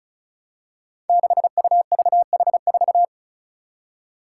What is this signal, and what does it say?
Morse code, heard as a keyed tone, about 36 wpm, 700 Hz. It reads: 6UVH4
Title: Five Challenging Letter-Number Combinations 35wpm